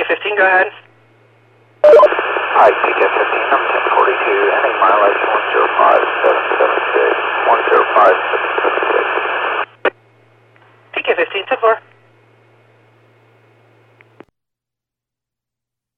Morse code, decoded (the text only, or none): none